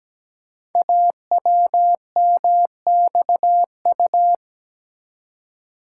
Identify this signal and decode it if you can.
Morse code, heard as a keyed tone, about 17 wpm, 700 Hz. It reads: AWMXU
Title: Code Groups - 5 Letters - 17wpm